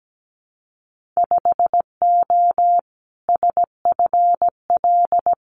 Morse code, heard as a keyed tone, about 17 wpm, 700 Hz. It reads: 5O SFL
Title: Winter Field Day Exchanges 17wpm